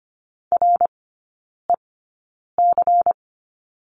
Morse code, heard as a keyed tone, about 25 wpm, spaced as for 9 wpm, 700 Hz. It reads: REC
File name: course / 25:10wpm / 093 Lesson 16e - Letter c - Three Characters 25wpm